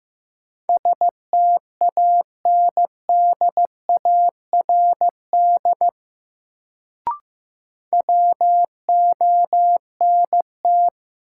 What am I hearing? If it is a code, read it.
Morse code, heard as a keyed tone, about 15 wpm, 700 Hz. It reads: STANDARD WONT
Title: Sentences from Top 1000 Words - Review Part 6 - New Word Drill - 15wpm